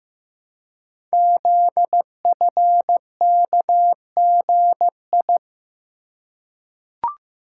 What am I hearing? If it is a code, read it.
Morse code, heard as a keyed tone, about 15 wpm, 700 Hz. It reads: ZFKGI
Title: Code Groups - 5 Letters - 15wpm